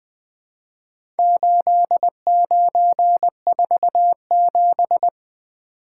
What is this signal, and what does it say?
Morse code, heard as a keyed tone, about 20 wpm, 700 Hz. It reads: 8947
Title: Four Letter-Number Combinations - 20wpm